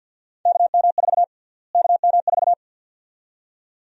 Morse code, heard as a keyed tone, about 48 wpm, 700 Hz. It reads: KM4 KM4
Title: USA Call Sign Prefixes - Repeated 2x - 50wpm